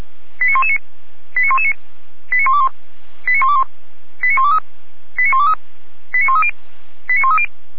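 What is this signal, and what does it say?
Five-tone selective call (ZVEI), heard as quick runs of five tones, tones of about 70 ms. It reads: 89119 89119 89121 89121 89123 89123 89130 89130
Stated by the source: Five-tone ZVEI selective call received on an SDR, from audio.f85360.feuerwehr-orlamuende.org (Feuerwehr Orlamünde fire brigade).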